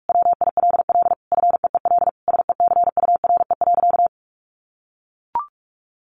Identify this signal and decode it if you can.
Morse code, heard as a keyed tone, about 44 wpm, 700 Hz. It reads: WILL FEEL SECURE.
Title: Sentences from Top 1500 Words 45wpm